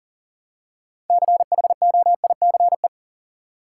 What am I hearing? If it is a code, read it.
Morse code, heard as a keyed tone, about 40 wpm, 700 Hz. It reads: CHOICE
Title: Six Letter Words - 40wpm